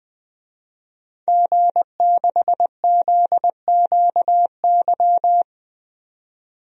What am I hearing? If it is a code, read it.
Morse code, heard as a keyed tone, about 20 wpm, 700 Hz. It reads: G6ZQY
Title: DXCC-100 Call Signs - 5 Characters - 20wpm